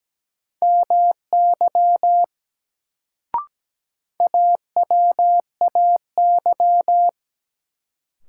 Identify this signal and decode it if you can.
Morse code, heard as a keyed tone, about 17 wpm, 700 Hz. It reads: MY AWAY
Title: Sets of 2 Words - New Word Drill - 17wpm